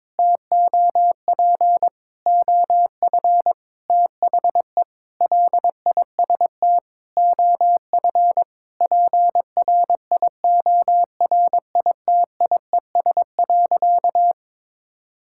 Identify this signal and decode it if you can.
Morse code, heard as a keyed tone, about 22 wpm, 700 Hz. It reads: TOP OF THE LIST OF PRIORITIES.
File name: Sentences with 7-Word Phrases - 22wpm